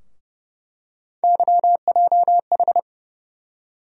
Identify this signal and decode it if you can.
Morse code, heard as a keyed tone, about 29 wpm, 700 Hz. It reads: YJH
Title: USA Call Sign Postfixes - 3 Letters 30wpm